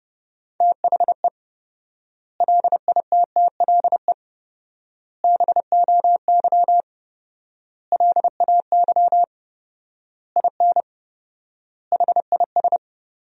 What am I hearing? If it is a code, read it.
Morse code, heard as a keyed tone, about 30 wpm, 700 Hz. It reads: THE LITTLE BOY LAY IN HIS
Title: 4x Word-Spacing - Sentences from Top 500 Words - 30wpm